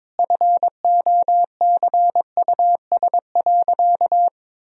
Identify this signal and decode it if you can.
Morse code, heard as a keyed tone, about 22 wpm, 700 Hz. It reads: FOCUS.
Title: Sentences from Top 1000 Words - Review Part 4 - 22wpm